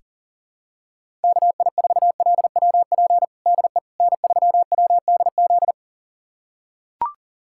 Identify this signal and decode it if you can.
Morse code, heard as a keyed tone, about 40 wpm, 700 Hz. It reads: KI4LWP DE N3WDZ
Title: USA Callsign DE Callsign 40wpm